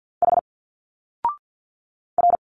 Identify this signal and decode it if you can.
Morse code, heard as a keyed tone, about 50 wpm, 700 Hz. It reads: H R
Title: Single Letters 50wpm